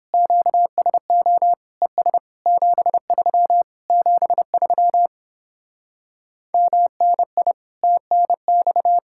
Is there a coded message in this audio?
Yes: Morse code from 0.1 to 9.0 s, heard as a keyed tone, about 29 wpm, 700 Hz. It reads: QSO ES 73 73 MNI TNX